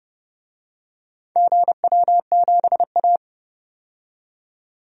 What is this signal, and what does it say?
Morse code, heard as a keyed tone, about 30 wpm, 700 Hz. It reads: GW7A
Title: DXCC-100 Call Signs - 4 Characters - 30wpm